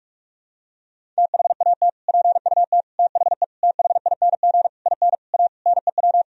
Morse code, heard as a keyed tone, about 46 wpm, 700 Hz. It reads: THAT PUT THE THING IN A NEW
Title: Sentences from Top 1000 Words - Review Part 1 - 45wpm